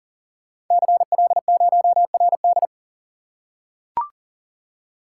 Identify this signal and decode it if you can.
Morse code, heard as a keyed tone, about 38 wpm, 690 Hz. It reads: CL0RD
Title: Code Groups - 5 Letter:Numbers - 40wpm